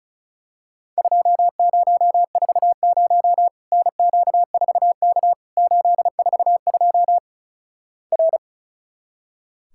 Morse code, heard as a keyed tone, about 36 wpm, 700 Hz. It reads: 2040 NQ4K 842 R
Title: Worked All Europe DX Contest - QTC Exchange - 35wpm